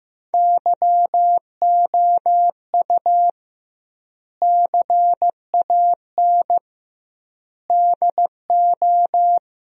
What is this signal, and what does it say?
Morse code, heard as a keyed tone, about 15 wpm, 700 Hz. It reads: YOU CAN DO
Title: 2x Word-Spacing - Sentences from Top 100 Words - 15wpm